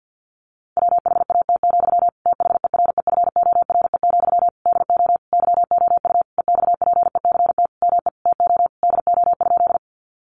Morse code, heard as a keyed tone, about 50 wpm, 700 Hz. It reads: WHAT, THEREFORE, DO YOU EXPECT ME TO DO?
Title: Sentences from Top 1100 Words 50wpm